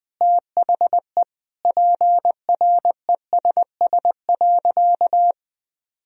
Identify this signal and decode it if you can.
Morse code, heard as a keyed tone, about 20 wpm, 700 Hz. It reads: THE PRESS.
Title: Sentences from Top 1000 Words - Review Part 1 - 20wpm